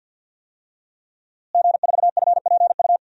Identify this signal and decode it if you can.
Morse code, heard as a keyed tone, about 45 wpm, 700 Hz. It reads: G4FPU
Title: DXCC-100 Call Signs - 5 Characters - Repeated 2x - 50wpm